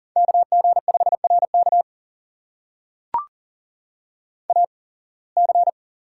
Morse code, heard as a keyed tone, about 40 wpm, spaced as for 26 wpm, 700 Hz. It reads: KG5RK AC
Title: USA Call Signs - 8x Character Spacing - 40wpm